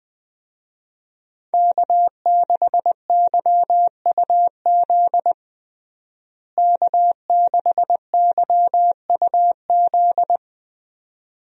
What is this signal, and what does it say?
Morse code, heard as a keyed tone, about 20 wpm, 700 Hz. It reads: K6YUZ K6YUZ
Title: USA Call Signs - 5 Characters - Repeated 2x - 20wpm